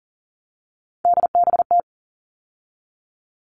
Morse code, heard as a keyed tone, about 40 wpm, 700 Hz. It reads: DBT
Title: All Cut Numbers - 3-Digit 40wpm